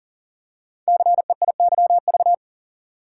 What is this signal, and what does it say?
Morse code, heard as a keyed tone, about 42 wpm, 700 Hz. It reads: CEIYV
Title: Code Groups - 5 Letter:Numbers - 40wpm